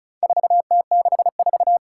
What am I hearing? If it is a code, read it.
Morse code, heard as a keyed tone, about 36 wpm, 700 Hz. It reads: 4T64
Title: Four Challenging Letter-Number Combinations 35wpm